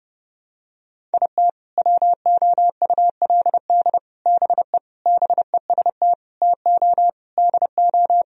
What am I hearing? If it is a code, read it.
Morse code, heard as a keyed tone, about 32 wpm, 700 Hz. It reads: IT WOULD BE BEST TO DO
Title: Sentences from Top 1000 Words - Review Part 2 - 30wpm